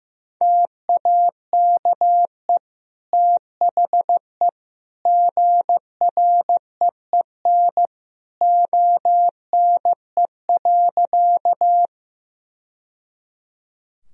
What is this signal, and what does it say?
Morse code, heard as a keyed tone, about 16 wpm, 700 Hz. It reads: TAKE THE GREEN ONE.
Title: Sentences from Top 1000 Words - Review Part 2 - 15wpm